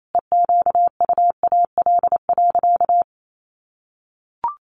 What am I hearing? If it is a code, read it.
Morse code, heard as a keyed tone, about 28 wpm, 700 Hz. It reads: EQUAL.
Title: Sentences from Top 1400 Words 28wpm